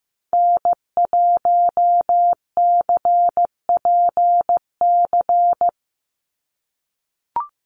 Morse code, heard as a keyed tone, about 15 wpm, 700 Hz. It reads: N1CPC
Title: USA Call Signs 15wpm